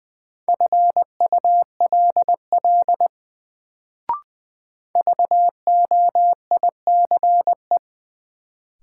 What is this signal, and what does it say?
Morse code, heard as a keyed tone, about 20 wpm, 700 Hz. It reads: FULL VOICE
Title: Sentences from Top 500 Words - New Word Drill - 20wpm